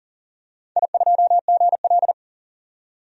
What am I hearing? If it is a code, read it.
Morse code, heard as a keyed tone, about 40 wpm, 700 Hz. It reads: I2GL
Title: DXCC-100 Call Signs - 4 Characters - 40wpm